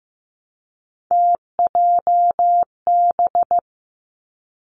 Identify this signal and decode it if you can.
Morse code, heard as a keyed tone, about 15 wpm, 700 Hz. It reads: TJB
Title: Three Challenging Letter-Number Combinations 15wpm